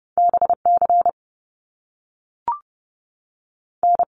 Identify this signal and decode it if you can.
Morse code, heard as a keyed tone, about 30 wpm, 700 Hz. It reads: BC N
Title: Canadian Provinces and Territories 30wpm